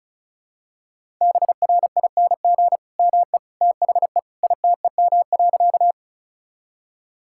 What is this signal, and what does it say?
Morse code, heard as a keyed tone, about 36 wpm, 700 Hz. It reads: BRING ME THE ITEM.
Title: Sentences from Top 1000 Words - Review Part 6 - 35wpm